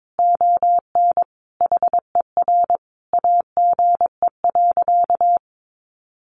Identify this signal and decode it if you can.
Morse code, heard as a keyed tone, about 22 wpm, 700 Hz. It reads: ON HER AGE.
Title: Sentences from Top 1300 Words 22wpm